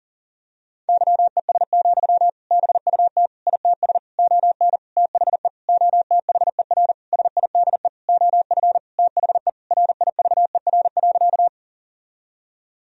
Morse code, heard as a keyed tone, about 40 wpm, 700 Hz. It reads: YES, BUT ITS ON THE OTHER SIDE OF THE RIVER.
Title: Sentences from Top 1000 Words - Review Part 1 - 40wpm